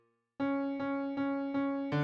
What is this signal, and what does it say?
There is no code.